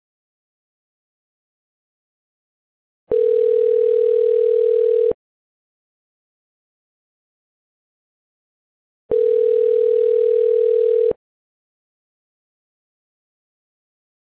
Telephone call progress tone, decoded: ringback tone